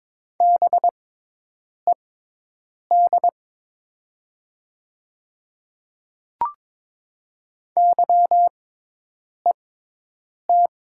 Morse code, heard as a keyed tone, about 22 wpm, spaced as for 8 wpm, 700 Hz. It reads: BED YET
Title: 6x Character Spacing - Top 500 Words - 22wpm